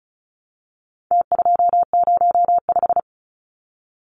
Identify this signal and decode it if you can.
Morse code, heard as a keyed tone, about 34 wpm, 700 Hz. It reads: T205
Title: Numbers - Contest Style - 4-Digit 35wpm